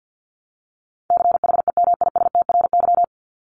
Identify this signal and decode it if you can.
Morse code, heard as a keyed tone, about 50 wpm, 700 Hz. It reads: CHEMISTRY
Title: Top 751-1000 Words 50wpm